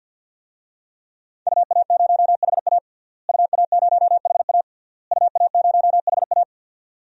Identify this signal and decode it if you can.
Morse code, heard as a keyed tone, about 49 wpm, 700 Hz. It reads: UA0HA UA0HA UA0HA
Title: DXCC-100 Call Signs - 5 Characters - Repeated 3x - 50wpm